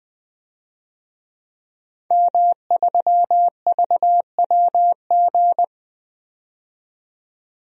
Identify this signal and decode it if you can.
Morse code, heard as a keyed tone, about 20 wpm, 700 Hz. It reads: M3VWG
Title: DXCC-100 Call Signs - 5 Characters - 20wpm